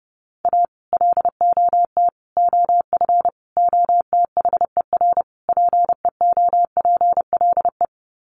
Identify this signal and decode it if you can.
Morse code, heard as a keyed tone, about 30 wpm, 700 Hz. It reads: A LOT OF OTHER PEOPLE
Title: Sets of 5 Words Encore2 30wpm